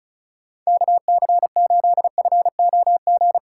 Morse code, heard as a keyed tone, about 36 wpm, 700 Hz. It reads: KC8FOG
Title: USA Call Signs - 2x Character Spacing - 35wpm